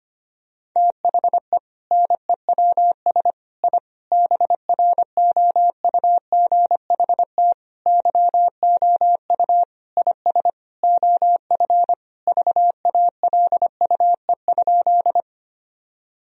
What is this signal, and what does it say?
Morse code, heard as a keyed tone, about 25 wpm, 700 Hz. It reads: THE NEWS I BROUGHT YOU IS OF VALUE?
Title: Sentences from Top 1000 Words - Review Part 3 - 25wpm